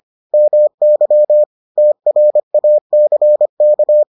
Morse code, heard as a keyed tone, about 25 wpm, 600 Hz. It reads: MY TRACK